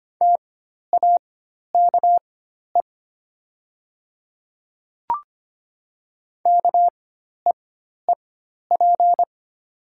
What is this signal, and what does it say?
Morse code, heard as a keyed tone, about 25 wpm, spaced as for 12 wpm, 690 Hz. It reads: TAKE KEEP
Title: 4x Character Spacing - Top 500 Words - 25wpm